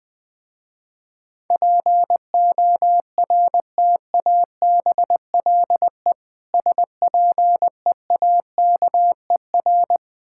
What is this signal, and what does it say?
Morse code, heard as a keyed tone, about 20 wpm, 700 Hz. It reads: PORTABLE SPEAKER
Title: Difficult Words with Hints - 20wpm